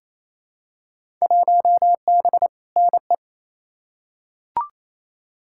Morse code, heard as a keyed tone, about 28 wpm, 700 Hz. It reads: 1B NE
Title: Field Day Exchanges Encore - 28wpm